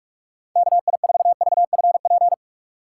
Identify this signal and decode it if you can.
Morse code, heard as a keyed tone, about 46 wpm, 700 Hz. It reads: KI4VFP
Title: USA Call Signs - 6 Characters - 4x Character Spacing - 45wpm